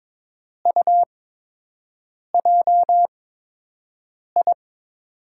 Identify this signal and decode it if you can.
Morse code, heard as a keyed tone, about 22 wpm, spaced as for 6 wpm, 700 Hz. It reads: UJI
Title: USA Call Signs - 6 Characters - 8x Character Spacing - 22wpm